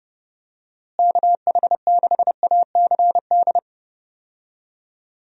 Morse code, heard as a keyed tone, about 32 wpm, 700 Hz. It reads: KH6ACD
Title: USA Call Signs - 6 Characters - Repeated 3x - 30wpm